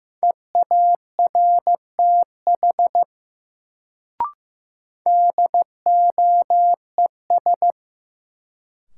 Morse code, heard as a keyed tone, about 15 wpm, 700 Hz. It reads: EARTH DOES
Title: Sentences from Top 200 Words - New Word Drill - 15wpm